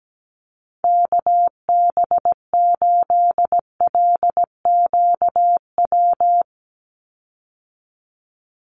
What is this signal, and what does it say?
Morse code, heard as a keyed tone, about 17 wpm, 700 Hz. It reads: KB8LQW
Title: USA Call Signs 17wpm